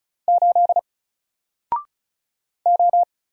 Morse code, heard as a keyed tone, about 35 wpm, 700 Hz. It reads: 8 O
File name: Single-Letter-Number-35wpm